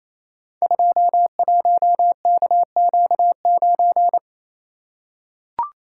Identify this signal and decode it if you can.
Morse code, heard as a keyed tone, about 28 wpm, 700 Hz. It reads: 21KQ9
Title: Code Groups - 5 Letter:Numbers - 28wpm